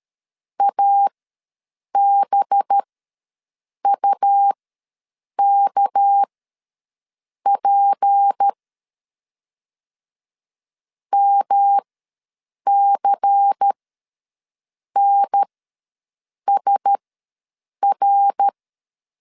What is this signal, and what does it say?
Morse code, heard as a keyed tone, about 13 wpm, spaced as for 6 wpm, 800 Hz. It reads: ABUKP MCNSR
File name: LEZIONI / 7 GF9Z / 7 GF9 ws 12wpm 3344